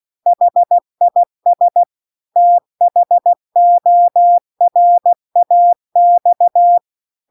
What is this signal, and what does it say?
Morse code, heard as a keyed tone, about 16 wpm, 700 Hz. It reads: HIS THORAX
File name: CW-The-Big-Twoheaded-River-2-16WPM